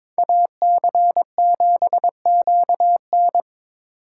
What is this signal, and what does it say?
Morse code, heard as a keyed tone, about 25 wpm, 700 Hz. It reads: AC7QN
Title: Speed Racing - USA 2x2 Call Signs 22wpm